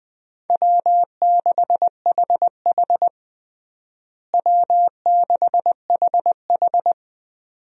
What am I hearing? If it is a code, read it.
Morse code, heard as a keyed tone, about 21 wpm, 700 Hz. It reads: W6HH W6HH
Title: USA Call Signs - 4 Characters - Repeated 2x - 20wpm